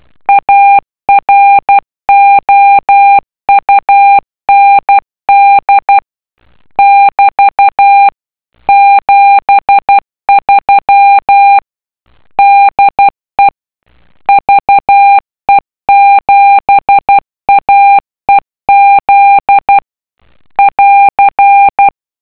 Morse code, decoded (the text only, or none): AROUND <BT> 73 DE VE7AEZ <AR>